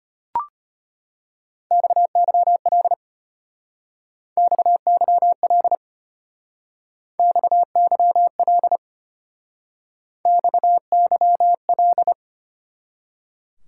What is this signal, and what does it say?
Morse code, heard as a keyed tone, about 31 wpm, 700 Hz. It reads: XYL XYL XYL XYL
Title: Speed-Racing Top 90 QSO Elements 25wpm